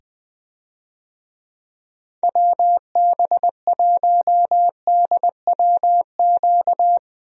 Morse code, heard as a keyed tone, about 20 wpm, 700 Hz. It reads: WB1DWQ
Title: USA Call Signs - 6 Characters - 20wpm